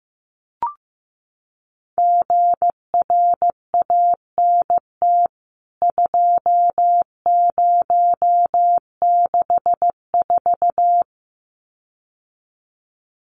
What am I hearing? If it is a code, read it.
Morse code, heard as a keyed tone, about 15 wpm, 700 Hz. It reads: GRANT 2064
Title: CWT Exchange 15wpm